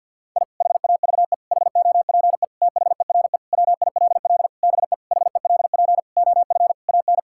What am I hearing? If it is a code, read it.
Morse code, heard as a keyed tone, about 52 wpm, 700 Hz. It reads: I HAVE HOPE THERE WILL BE HELP OF AN
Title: Sentences from Top 1000 Words - Review Part 5 - 50wpm